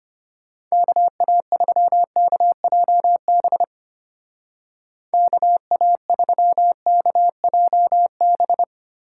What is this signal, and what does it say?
Morse code, heard as a keyed tone, about 32 wpm, 700 Hz. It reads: KA3KJB KA3KJB